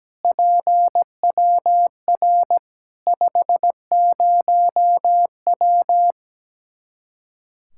Morse code, heard as a keyed tone, about 17 wpm, 700 Hz. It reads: PWR 50W
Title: K1USN SST QSO Practice 17wpm